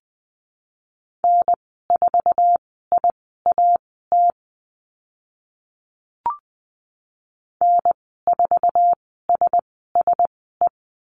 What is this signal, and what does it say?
Morse code, heard as a keyed tone, about 20 wpm, spaced as for 15 wpm, 700 Hz. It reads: N4IAT N4SSE
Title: 076 Lesson 13g - Letter L - Call Signs 20wpm